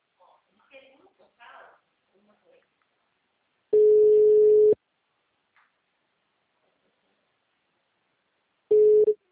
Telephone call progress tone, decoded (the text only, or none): ringback tone